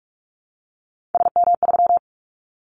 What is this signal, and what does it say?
Morse code, heard as a keyed tone, about 46 wpm, 700 Hz. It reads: SM3
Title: Callsign Prefixes for Top 25 DXCC - Repeated 3x - 45wpm